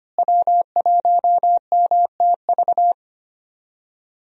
Morse code, heard as a keyed tone, about 25 wpm, 700 Hz. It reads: W1MTV
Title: USA Call Signs - 5 Characters - Repeated 3x - 25wpm